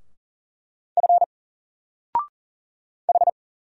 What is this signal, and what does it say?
Morse code, heard as a keyed tone, about 38 wpm, 700 Hz. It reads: F H
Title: Single Character (Letter:Number:Punctuation:Prosign) Alt 40wpm